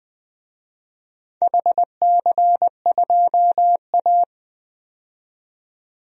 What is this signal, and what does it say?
Morse code, heard as a keyed tone, about 21 wpm, 700 Hz. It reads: HC2A
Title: DXCC-100 Call Signs - 4 Characters - 20wpm